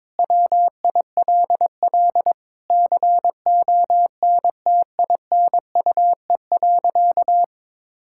Morse code, heard as a keyed tone, about 22 wpm, 700 Hz. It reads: WILL CONTINUE.
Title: Sentences from Top 1000 Words - Review Part 6 - 22wpm